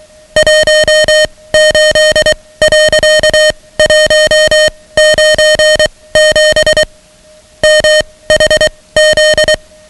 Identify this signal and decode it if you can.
Morse code, heard as a keyed tone, about 23 wpm, spaced as for 17 wpm, 620 Hz, with noise. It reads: 18.197 MHZ
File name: ham radio / CodeQuick / Code Quick 10